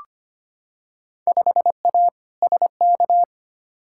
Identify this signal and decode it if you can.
Morse code, heard as a keyed tone, about 25 wpm, 700 Hz. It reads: 5A SK
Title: Field Day Exchanges Encore - 25wpm